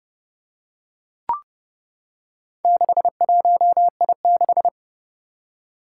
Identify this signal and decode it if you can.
Morse code, heard as a keyed tone, about 32 wpm, 700 Hz. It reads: 61I6
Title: Four Challenging Letter-Number Combinations 30wpm